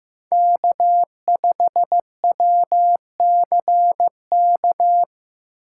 Morse code, heard as a keyed tone, about 15 wpm, 700 Hz. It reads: K5WCK
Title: USA Call Signs - 5 Characters - 3x Character Spacing - 15wpm